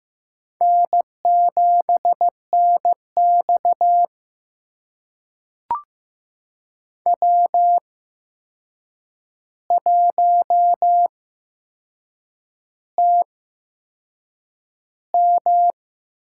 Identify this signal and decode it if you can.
Morse code, heard as a keyed tone, about 15 wpm, spaced as for 6 wpm, 700 Hz. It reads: N7NX W1TM
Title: USA Call Signs - 4 Characters - 8x Character Spacing - 15wpm